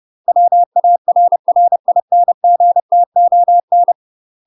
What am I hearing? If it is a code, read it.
Morse code, heard as a keyed tone, about 30 wpm, 700 Hz. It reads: WARRINGTON
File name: UK MASTER QSOs TRAINING / 30WPM / UK MASTER QSOs 0043